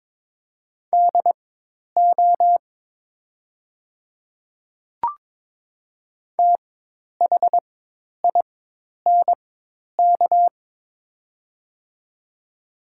Morse code, heard as a keyed tone, about 21 wpm, spaced as for 10 wpm, 700 Hz. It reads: DO THINK
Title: 4x Character Spacing - Top 200 Words - 22wpm